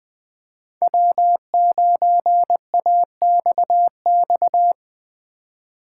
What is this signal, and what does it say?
Morse code, heard as a keyed tone, about 20 wpm, 700 Hz. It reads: W9AXX